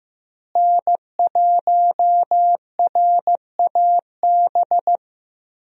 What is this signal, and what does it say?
Morse code, heard as a keyed tone, about 15 wpm, 700 Hz. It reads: N1RAB
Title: USA Call Signs - 5 Characters - Repeated 2x - 15wpm